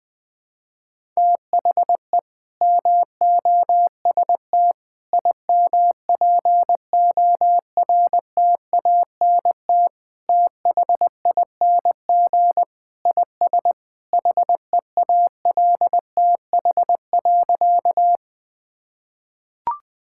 Morse code, heard as a keyed tone, about 20 wpm, 700 Hz. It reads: THE MOST IMPORTANT THING IS HEALTH.
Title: Sentences with 3-Word Phrases - Spoken Beginning - 20wpm